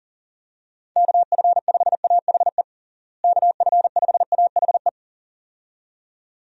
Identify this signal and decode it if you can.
Morse code, heard as a keyed tone, about 40 wpm, 700 Hz. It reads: KF5AHE KF5AHE
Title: USA Call Signs - 6 Characters - Repeated 2x - 40wpm